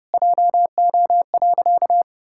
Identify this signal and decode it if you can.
Morse code, heard as a keyed tone, about 30 wpm, 700 Hz. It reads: JO.
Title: Pangram Challenge v2 - 30wpm